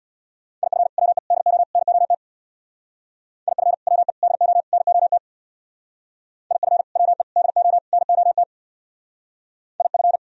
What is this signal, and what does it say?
Morse code, heard as a keyed tone, about 60 wpm, 700 Hz. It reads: IF WE DO NOT IF WE DO NOT IF WE DO NOT IF
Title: Speed-Racing - Sets of 4 Words 50wpm